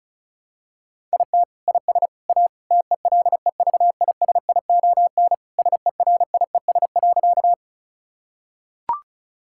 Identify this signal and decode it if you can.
Morse code, heard as a keyed tone, about 34 wpm, 700 Hz. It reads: IT IS A TELEVISION SERIES.
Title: Sentences from Top 1000 Words - Review Part 5 - 35wpm